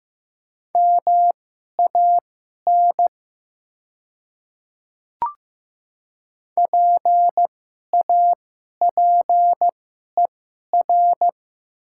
Morse code, heard as a keyed tone, about 15 wpm, spaced as for 11 wpm, 700 Hz. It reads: MAN PAPER
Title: 2x Character Spacing - Top 300 Words - 15wpm